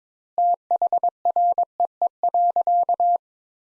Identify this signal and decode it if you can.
Morse code, heard as a keyed tone, about 22 wpm, 700 Hz. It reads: THREE.
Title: Sentences Top 200 Words - 22wpm